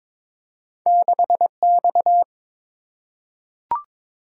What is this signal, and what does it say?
Morse code, heard as a keyed tone, about 22 wpm, 700 Hz. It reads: 6X
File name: practice / 22wpm / Two Letter-Number Combinations - 22wpm